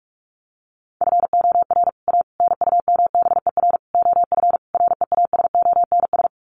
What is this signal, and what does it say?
Morse code, heard as a keyed tone, about 44 wpm, 700 Hz. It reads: FOR A NUMBER OF REASONS
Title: Sets of 5 Words Encore2 45wpm